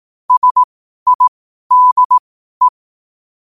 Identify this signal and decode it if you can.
Morse code, heard as a keyed tone, about 18 wpm, spaced as for 12 wpm, 1000 Hz. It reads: SIDE
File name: qotdfiles / 12wpm